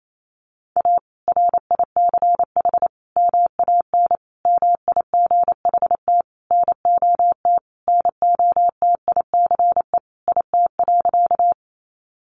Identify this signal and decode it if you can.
Morse code, heard as a keyed tone, about 28 wpm, 700 Hz. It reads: A RICH MAN MIGHT NOT NOTICE IT.
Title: Sentences from Top 1100 Words 28wpm